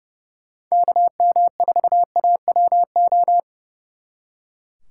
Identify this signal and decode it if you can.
Morse code, heard as a keyed tone, about 35 wpm, 700 Hz. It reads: KM4AWO
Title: Speed Racing - USA 2x3 Call Signs 30wpm